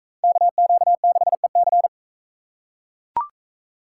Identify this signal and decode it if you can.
Morse code, heard as a keyed tone, about 41 wpm, 700 Hz. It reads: KQ6EC